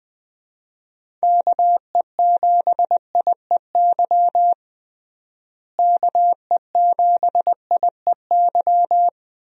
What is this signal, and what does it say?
Morse code, heard as a keyed tone, about 20 wpm, 700 Hz. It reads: KE7IEY KE7IEY
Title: USA Call Signs - 6 Characters - Repeated 3x - 20wpm